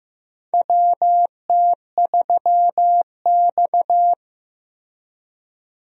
Morse code, heard as a keyed tone, about 16 wpm, 700 Hz. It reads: WT3X
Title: USA Call Signs - 4 Characters - Repeated 2x - 15wpm